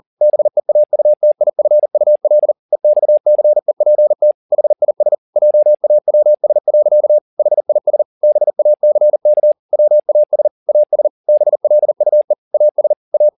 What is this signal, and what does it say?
Morse code, heard as a keyed tone, about 40 wpm, 600 Hz. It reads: BEAUTIFUL EXCEPT HIS JAWS. HIS BACK WAS AS BLUE AS A
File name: the old man and the sea 2 40wpm